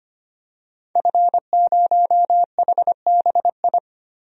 Morse code, heard as a keyed tone, about 25 wpm, 700 Hz. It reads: F0HBI
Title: DXCC-100 Call Signs - 5 Characters - 25wpm